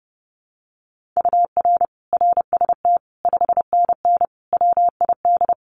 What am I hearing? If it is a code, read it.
Morse code, heard as a keyed tone, about 30 wpm, 700 Hz. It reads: UR RST 5NN WID
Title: Common QSO Phrases 30wpm